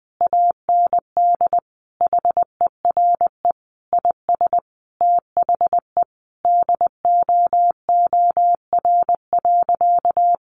Morse code, heard as a keyed tone, about 20 wpm, 700 Hz. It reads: AND HERE IS THE DOOR.
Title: Sentences from Top 1100 Words 20wpm